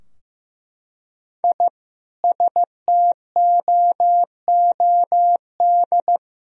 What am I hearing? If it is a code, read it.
Morse code, heard as a keyed tone, about 15 wpm, 700 Hz. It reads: I STOOD
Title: Sentences from Top 1000 Words - Review Part 2 - 15wpm